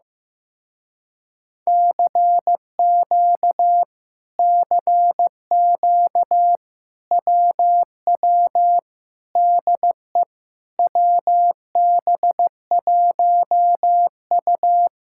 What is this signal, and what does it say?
Morse code, heard as a keyed tone, about 15 wpm, 700 Hz. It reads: CQ CQ WW DE WB1U
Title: Speed-Racing Common QSO Phrases Encore - 15wpm